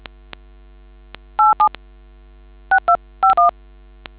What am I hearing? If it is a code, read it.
DTMF, heard as telephone touch tones, tones of about 110 ms. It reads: 87 6251